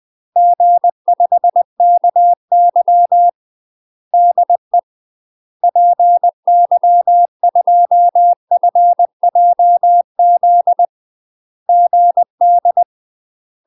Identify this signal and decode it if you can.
Morse code, heard as a keyed tone, about 20 wpm, 700 Hz. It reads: G5KY DE PY2FJZ GD
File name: UK MASTER QSOs TRAINING / 20WPM / UK MASTER QSOs 0037